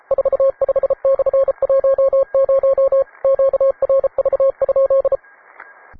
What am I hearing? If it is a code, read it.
Morse code, heard as a keyed tone, about 34 wpm, 540 Hz. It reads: 45/10 QRV?